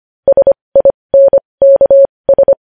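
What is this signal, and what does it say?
Morse code, heard as a keyed tone, about 25 wpm, spaced as for 20 wpm, 550 Hz. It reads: SINKS
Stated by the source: News headlines sent in Morse code at 20.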